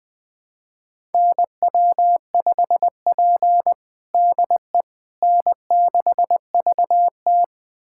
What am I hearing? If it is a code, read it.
Morse code, heard as a keyed tone, about 21 wpm, 700 Hz. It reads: NW5P DE N6VT
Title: USA Callsign DE Callsign - 4 Characters - 20wpm